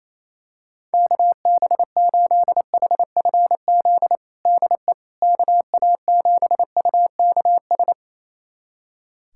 Morse code, heard as a keyed tone, about 28 wpm, 700 Hz. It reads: KB8HFZ DE KA7UKS
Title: USA Callsign DE Callsign - 6 Characters - 28wpm